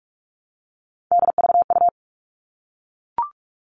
Morse code, heard as a keyed tone, about 44 wpm, 700 Hz. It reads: DVU
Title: All Cut Numbers - 3-Digit 45wpm